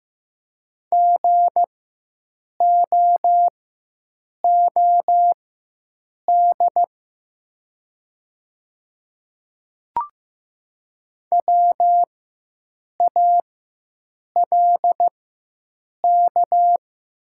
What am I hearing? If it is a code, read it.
Morse code, heard as a keyed tone, about 15 wpm, spaced as for 7 wpm, 700 Hz. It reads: GOOD WALK
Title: 4x Character Spacing - Top 300 Words - 15wpm